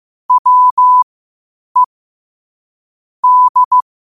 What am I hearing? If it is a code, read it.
Morse code, heard as a keyed tone, about 15 wpm, spaced as for 8 wpm, 1000 Hz. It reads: WE D
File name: qotdfiles / 8wpm